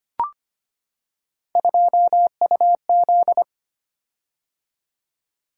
Morse code, heard as a keyed tone, about 25 wpm, 700 Hz. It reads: 2UZ
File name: Three Letter-Number Combinations - 25wpm